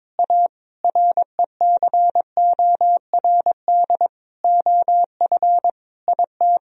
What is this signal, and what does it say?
Morse code, heard as a keyed tone, about 22 wpm, 700 Hz. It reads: A RECORD OF IT
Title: Sentences from Top 1000 Words - Review Part 3 - 22wpm